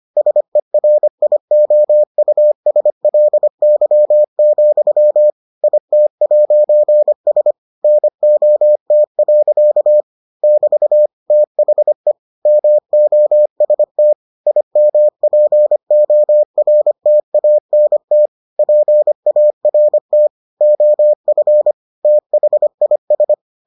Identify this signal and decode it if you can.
Morse code, heard as a keyed tone, about 25 wpm, 600 Hz. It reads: SERIOUSLY, IT'S NOT. <BT> THE MOST IMPORTANT PART OF THIS